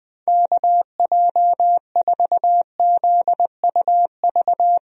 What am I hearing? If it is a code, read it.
Morse code, heard as a keyed tone, about 20 wpm, 700 Hz. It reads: KJ4ZUV
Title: USA Call Signs - 6 Characters - Repeated 2x - 20wpm